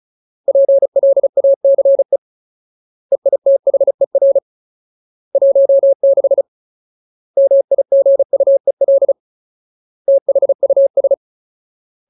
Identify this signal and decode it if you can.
Morse code, heard as a keyed tone, about 34 wpm, 550 Hz. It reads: PLACE EITHER 16 MIGUEL THUS
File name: Ultimate Word List / 35WPM Ultimate